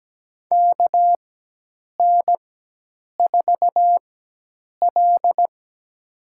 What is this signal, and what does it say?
Morse code, heard as a keyed tone, about 17 wpm, spaced as for 8 wpm, 700 Hz. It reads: KN4L
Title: USA Call Signs - 4x Character Spacing - 17wpm